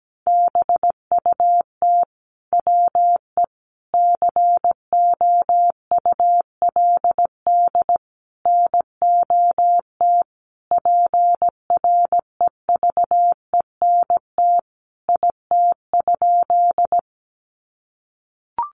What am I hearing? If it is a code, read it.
Morse code, heard as a keyed tone, about 17 wpm, 700 Hz. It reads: BUT WE COULD NOT PREVENT IT?
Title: Sentences from Top 1100 Words 17wpm